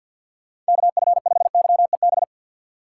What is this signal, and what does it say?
Morse code, heard as a keyed tone, about 52 wpm, 700 Hz. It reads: KF5YEB
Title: USA Call Signs - 6 Characters - 4x Character Spacing - 50wpm